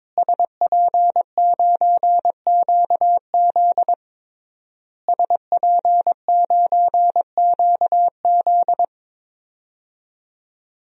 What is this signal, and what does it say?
Morse code, heard as a keyed tone, about 22 wpm, 700 Hz. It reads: SP9QZ SP9QZ
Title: DXCC-100 Call Signs - 5 Characters - Repeated 3x - 22wpm